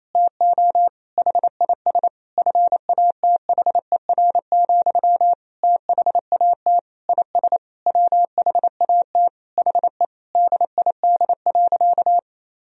Morse code, heard as a keyed tone, about 28 wpm, 700 Hz. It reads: TO HIS FATHER, THAT IS WHAT HE DID.
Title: Sentences from Top 1000 Words - Review Part 4 - 28wpm